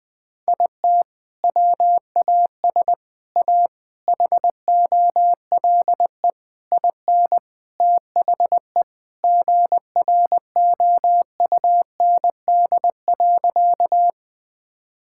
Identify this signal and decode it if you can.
Morse code, heard as a keyed tone, about 20 wpm, 700 Hz. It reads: IT WAS A HOLE IN THE GROUND.
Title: Sentences from Top 1600 Words 20wpm